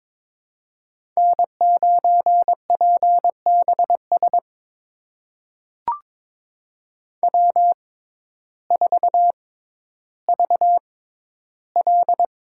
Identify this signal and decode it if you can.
Morse code, heard as a keyed tone, about 22 wpm, spaced as for 12 wpm, 700 Hz. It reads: N9PBS W4VL
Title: USA Call Signs - 5 Characters - 6x Character Spacing - 22wpm